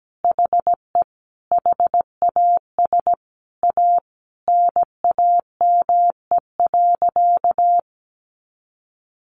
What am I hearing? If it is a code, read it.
Morse code, heard as a keyed tone, about 17 wpm, 700 Hz. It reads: HE HAS A NAME.
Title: Sentences from Top 500 Words 17wpm